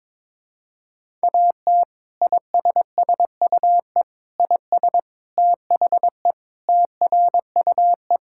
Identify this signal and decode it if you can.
Morse code, heard as a keyed tone, about 21 wpm, 700 Hz. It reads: AT ISSUE IS THE TRUE
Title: Sentences from Top 1000 Words - Review Part 6 - 22wpm